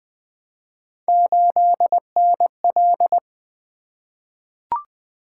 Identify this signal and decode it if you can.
Morse code, heard as a keyed tone, about 20 wpm, 700 Hz. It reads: 8NL
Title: Three Letter-Number Combinations - 20wpm